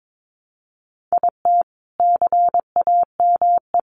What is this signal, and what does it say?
Morse code, heard as a keyed tone, about 22 wpm, 700 Hz. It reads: IT CAME
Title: Sentences from Top 1300 Words 22wpm